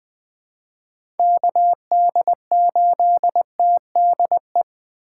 Morse code, heard as a keyed tone, about 21 wpm, 700 Hz. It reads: KD8TDE
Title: USA Call Signs - 6 Characters - 2x Character Spacing - 20wpm